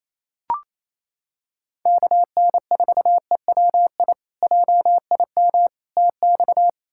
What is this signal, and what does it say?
Morse code, heard as a keyed tone, about 28 wpm, 700 Hz. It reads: KN4EWI JIM TX
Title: K1USN SST QSO Practice - 28wpm